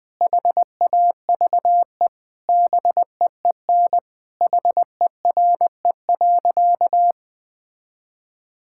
Morse code, heard as a keyed tone, about 20 wpm, 700 Hz. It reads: HAVE BEEN HERE.
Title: Sentences with 4-Word Phrases - Spoken Beginning - 20wpm